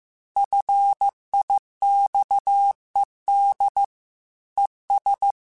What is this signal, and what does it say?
Morse code, heard as a keyed tone, about 15 wpm, 800 Hz. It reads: FIXED ES
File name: QSOs / QSOs-mp3s / WA7QOX 1@13